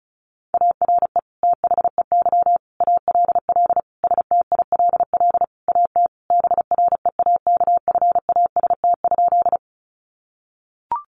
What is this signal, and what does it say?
Morse code, heard as a keyed tone, about 34 wpm, 700 Hz. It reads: ARE THEY ALL STILL AT BREAKFAST?
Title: Sentences from Top 1500 Words 35wpm